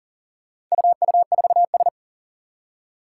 Morse code, heard as a keyed tone, about 40 wpm, 700 Hz. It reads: UU4S
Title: Four Challenging Letter-Number Combinations 40wpm